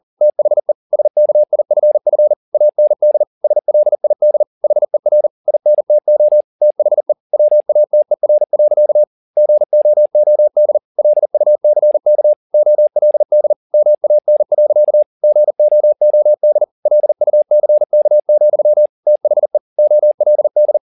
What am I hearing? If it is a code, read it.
Morse code, heard as a keyed tone, about 40 wpm, 600 Hz. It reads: THE SKIFF AND SLID HER INTO THE WATER. GOOD LUCK OLD MAN. GOOD LUCK, THE OLD